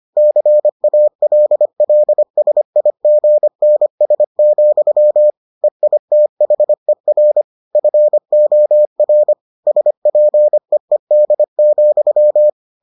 Morse code, heard as a keyed tone, about 25 wpm, 600 Hz. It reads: CALLSIGNS, EITHER FOR SPEED,